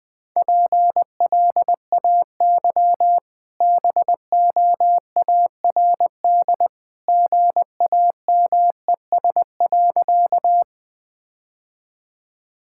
Morse code, heard as a keyed tone, about 20 wpm, 700 Hz. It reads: PLAY BOARD GAMES.
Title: Easy Sentences with Hints - 20wpm